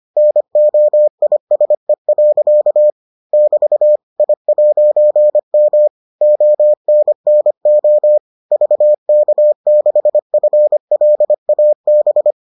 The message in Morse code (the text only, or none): NOISE. <BT> I'M ONNO VK6FLAB